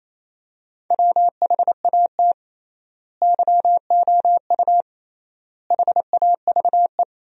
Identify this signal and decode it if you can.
Morse code, heard as a keyed tone, about 28 wpm, 700 Hz. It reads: WHAT YOU HAVE
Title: 3x Word-Spacing - Sentences from Top 200 Words - 28wpm